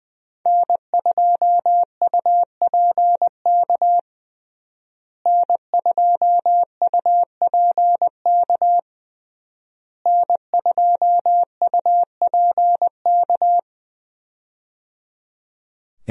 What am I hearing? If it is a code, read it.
Morse code, heard as a keyed tone, about 20 wpm, 700 Hz. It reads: N2UPK N2UPK N2UPK
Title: USA Call Signs - 5 Characters - Repeated 3x - 20wpm